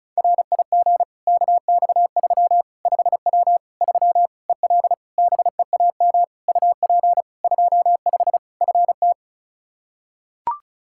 Morse code, heard as a keyed tone, about 36 wpm, 700 Hz. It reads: RIG KX3 5W 3 EL BEAM UP 25 FT
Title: K1USN SST QSO Practice 35wpm